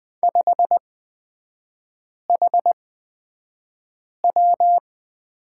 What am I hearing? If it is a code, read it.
Morse code, heard as a keyed tone, about 20 wpm, spaced as for 5 wpm, 700 Hz. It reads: 5HW
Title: 224 Lesson 36e - BT - Three Characters 20wpm